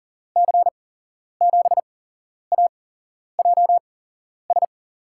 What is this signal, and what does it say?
Morse code, heard as a keyed tone, about 40 wpm, spaced as for 11 wpm, 700 Hz. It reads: C7AJS
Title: USA Call Signs - 6 Characters - 8x Character Spacing - 40wpm